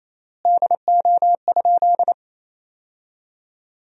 Morse code, heard as a keyed tone, about 28 wpm, 700 Hz. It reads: DO?
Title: 3x Word-Spacing - Sentences from Top 100 Words - 28wpm